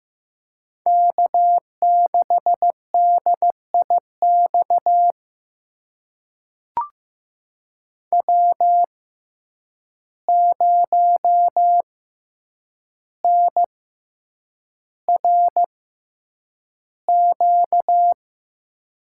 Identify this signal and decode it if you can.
Morse code, heard as a keyed tone, about 15 wpm, spaced as for 8 wpm, 700 Hz. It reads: K6DIX W0NRQ
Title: USA Call Signs - 6x Character Spacing - 15wpm